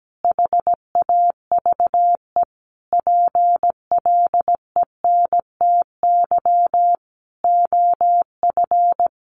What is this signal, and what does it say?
Morse code, heard as a keyed tone, about 17 wpm, 700 Hz. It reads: HAVE PLENTY OF
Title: Sentences from Top 1300 Words 17wpm